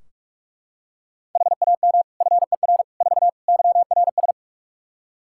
Morse code, heard as a keyed tone, about 43 wpm, 700 Hz. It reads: HAM FER 4 YRS
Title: Practice QSO - Exchange 3 - More Info - 45wpm